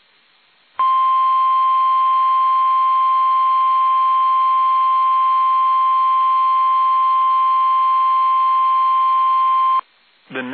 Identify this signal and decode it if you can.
Emergency Alert System (SAME) tones (the 1050 Hz attention tone), with speech. It is heard from 0.7 to 9.9 s.